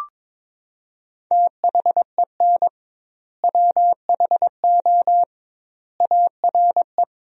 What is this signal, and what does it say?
Morse code, heard as a keyed tone, about 22 wpm, 700 Hz. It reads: THEN WHO ARE
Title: 2x Word-Spacing - Sentences from Top 100 Words - 22wpm